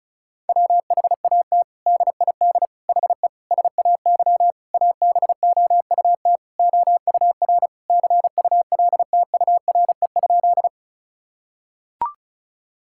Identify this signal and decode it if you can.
Morse code, heard as a keyed tone, about 34 wpm, 700 Hz. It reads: WHAT DID HE SAY ABOUT OUR CULTURE?
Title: Sentences from Top 1000 Words - Review Part 6 - 35wpm